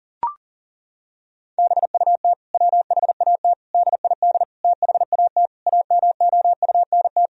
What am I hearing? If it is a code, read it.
Morse code, heard as a keyed tone, about 40 wpm, 700 Hz. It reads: BUT WHAT DID THAT AMOUNT
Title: Sentences from Top 1000 Words - Review Part 4 - 40wpm